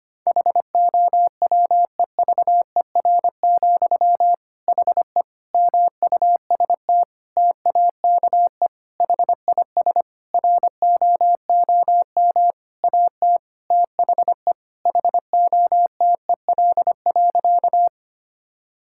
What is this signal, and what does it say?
Morse code, heard as a keyed tone, about 25 wpm, 700 Hz. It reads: HOWEVER, HE MUST TAKE HIS ROOM AT THE HOTEL.
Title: Sentences from Top 1600 Words 25wpm